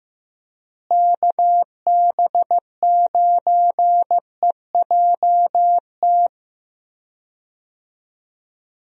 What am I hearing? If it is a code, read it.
Morse code, heard as a keyed tone, about 15 wpm, 700 Hz. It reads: KB9EJT
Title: USA Call Signs - 6 Characters - Repeated 3x - 15wpm